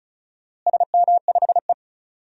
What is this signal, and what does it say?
Morse code, heard as a keyed tone, about 36 wpm, 700 Hz. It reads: SM5E
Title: DXCC-100 Call Signs - 4 Characters - Repeated 2x - 35wpm